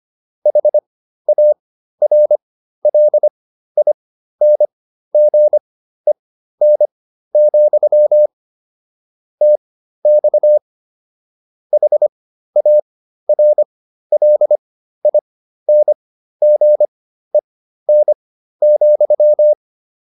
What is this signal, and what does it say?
Morse code, heard as a keyed tone, about 25 wpm, spaced as for 13 wpm, 600 Hz. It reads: HARLINGEN, TX HARLINGEN,